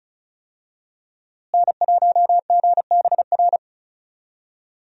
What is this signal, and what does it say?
Morse code, heard as a keyed tone, about 36 wpm, 700 Hz. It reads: N1GBR
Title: USA Call Signs - 5 Characters - Repeated 3x - 35wpm